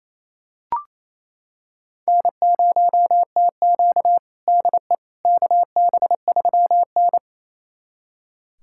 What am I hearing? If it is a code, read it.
Morse code, heard as a keyed tone, about 28 wpm, 700 Hz. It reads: N0TQ DE KB3N
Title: USA Callsign DE Callsign - 4 Characters - 28wpm